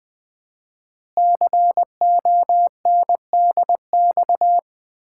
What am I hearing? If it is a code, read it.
Morse code, heard as a keyed tone, about 20 wpm, 700 Hz. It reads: CONDX
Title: Common QSO Phrases - 4x Word-Spacing - 20wpm